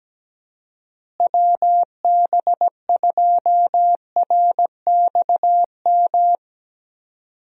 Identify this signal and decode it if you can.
Morse code, heard as a keyed tone, about 17 wpm, 700 Hz. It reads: WB2RXM
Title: USA Call Signs - 6 Characters - Repeated 2x - 17wpm